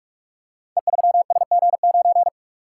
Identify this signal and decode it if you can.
Morse code, heard as a keyed tone, about 44 wpm, 700 Hz. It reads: E3SG9
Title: Code Groups - 5 Letter:Numbers - 45wpm